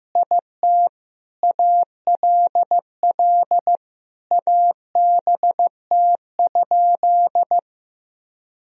Morse code, heard as a keyed tone, about 15 wpm, 700 Hz. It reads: IT ALL ABT?
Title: Sentences from Top 100 Words and Abbreviations 15wpm